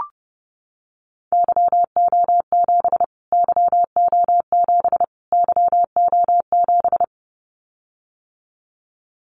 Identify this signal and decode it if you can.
Morse code, heard as a keyed tone, about 30 wpm, 700 Hz. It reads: YO7 YO7 YO7
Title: Callsign Prefixes for Top 25 DXCC - Repeated 3x - 30wpm